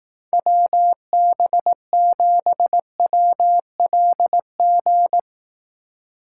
Morse code, heard as a keyed tone, about 18 wpm, 700 Hz. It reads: WB7WLG
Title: Speed Racing - USA Call Signs 15wpm